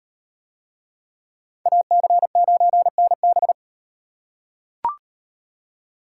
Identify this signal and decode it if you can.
Morse code, heard as a keyed tone, about 37 wpm, 700 Hz. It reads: AC9NB